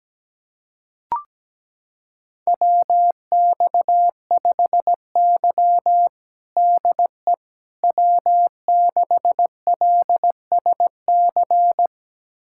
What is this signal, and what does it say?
Morse code, heard as a keyed tone, about 17 wpm, 700 Hz. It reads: WX5Y DE W6LSC
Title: USA Callsign DE Callsign 17wpm